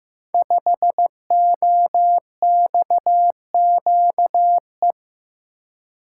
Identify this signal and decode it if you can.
Morse code, heard as a keyed tone, about 15 wpm, 700 Hz. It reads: 5OXQE
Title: Code Groups - 5 Letter:Numbers - 15wpm